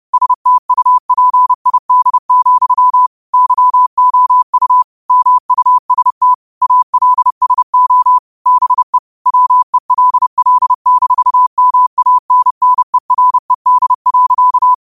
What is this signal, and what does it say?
Morse code, heard as a keyed tone, about 30 wpm, 1000 Hz. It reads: STUPID, YOU MUST ALSO BE WELL-MANNERED.